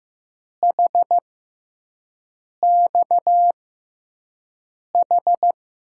Morse code, heard as a keyed tone, about 15 wpm, spaced as for 5 wpm, 700 Hz. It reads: HXH
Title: USA Call Signs - 6x Character Spacing - 15wpm